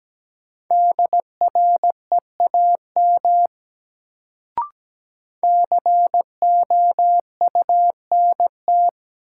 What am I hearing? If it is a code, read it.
Morse code, heard as a keyed tone, about 17 wpm, 700 Hz. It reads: DREAM COUNT
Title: Sentences from Top 1500 Words - New Word Drill Alt - 17wpm